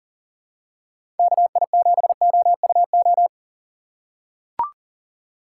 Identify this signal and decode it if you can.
Morse code, heard as a keyed tone, about 40 wpm, 700 Hz. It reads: KI7OUO
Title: USA Call Signs - 6 Characters - 2x Character Spacing - 40wpm